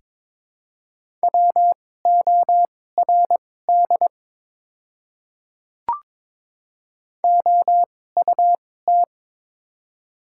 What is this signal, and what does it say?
Morse code, heard as a keyed tone, about 23 wpm, spaced as for 16 wpm, 700 Hz. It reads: WORD OUT
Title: 2x Character Spacing - Top 100 Words - 22wpm